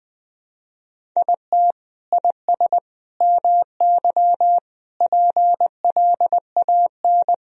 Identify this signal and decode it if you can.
Morse code, heard as a keyed tone, about 20 wpm, 700 Hz. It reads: IT IS MY PLAN
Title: Sentences from Top 1000 Words - Review Part 2 - 20wpm